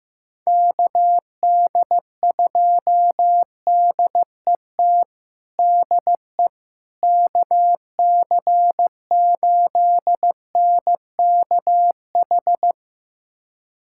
Morse code, heard as a keyed tone, about 15 wpm, 700 Hz. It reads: KD2DET DE KC8NKH
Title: USA Callsign DE Callsign - 6 Characters - 15wpm